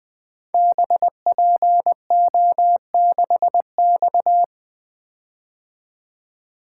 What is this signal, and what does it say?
Morse code, heard as a keyed tone, about 20 wpm, 700 Hz. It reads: BPO6X
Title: Code Groups - 5 Letter:Numbers - 20wpm